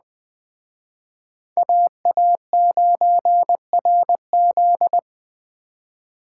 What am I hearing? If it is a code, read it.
Morse code, heard as a keyed tone, about 20 wpm, 700 Hz. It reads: AA9RZ
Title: Speed Racing - USA 2x2 Call Signs 17wpm